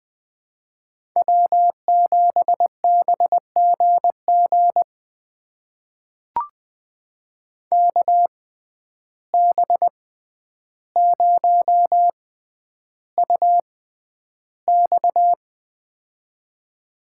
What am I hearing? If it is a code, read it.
Morse code, heard as a keyed tone, about 20 wpm, spaced as for 10 wpm, 700 Hz. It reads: W7BGG KB0UX
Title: USA Call Signs - 5 Characters - 6x Character Spacing - 20wpm